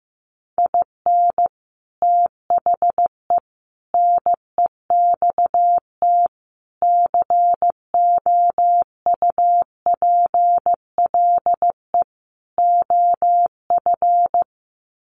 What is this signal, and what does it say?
Morse code, heard as a keyed tone, about 15 wpm, 700 Hz. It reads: IN THE NEXT COUPLE OF
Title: Sets of 5 Words Encore2 15wpm